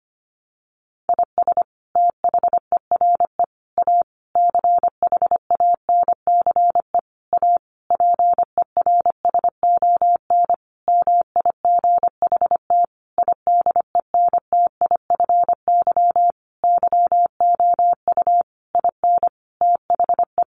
Morse code, heard as a keyed tone, about 25 wpm, 700 Hz. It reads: IS THERE A CHANCE A PERSON MIGHT IDENTIFY YOU IN THE
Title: Sentences from Top 900 Words 25wpm